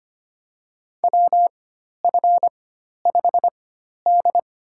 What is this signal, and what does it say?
Morse code, heard as a keyed tone, about 25 wpm, spaced as for 12 wpm, 700 Hz. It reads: WF5D
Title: USA Call Signs - 4 Characters - 4x Character Spacing - 25wpm